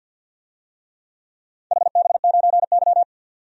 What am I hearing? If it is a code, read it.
Morse code, heard as a keyed tone, about 52 wpm, 700 Hz. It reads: HB9Y
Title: DXCC-100 Call Signs - 4 Characters - 50wpm